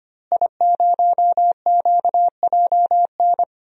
Morse code, heard as a keyed tone, about 25 wpm, 700 Hz. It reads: I0QJN
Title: DXCC-100 Call Signs - 5 Characters - Repeated 3x - 25wpm